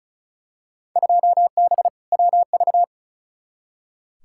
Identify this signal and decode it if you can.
Morse code, heard as a keyed tone, about 36 wpm, 700 Hz. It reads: 2B WV
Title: Field Day Exchanges Encore - 35wpm